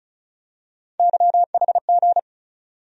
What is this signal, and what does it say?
Morse code, heard as a keyed tone, about 36 wpm, 700 Hz. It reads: YHG
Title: USA Call Sign Postfixes - 3 Letters 35wpm